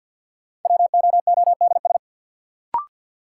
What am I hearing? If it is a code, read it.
Morse code, heard as a keyed tone, about 46 wpm, 700 Hz. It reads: WOODS
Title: Sentences from Top 1000 Words - Review Part 2 - New Word Drill - 50wpm